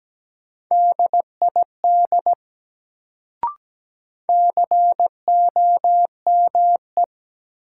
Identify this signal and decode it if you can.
Morse code, heard as a keyed tone, about 17 wpm, 700 Hz. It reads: DID COME
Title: Sentences from Top 100 Words - New Word Drill - 17wpm